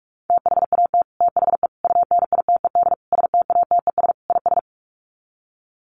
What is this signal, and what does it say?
Morse code, heard as a keyed tone, about 44 wpm, 700 Hz. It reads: THAT THE UNITED STATES IS
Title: Sets of 5 Words Encore 45wpm